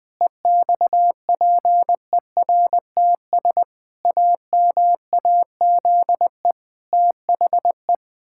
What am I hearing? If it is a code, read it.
Morse code, heard as a keyed tone, about 21 wpm, 700 Hz. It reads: EXPERTS AMAZE THE
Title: Pangram Challenge v2 - 20wpm